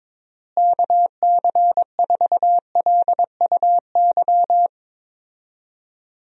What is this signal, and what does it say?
Morse code, heard as a keyed tone, about 21 wpm, 700 Hz. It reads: KC4LUY
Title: USA Call Signs - 6 Characters - Repeated 3x - 22wpm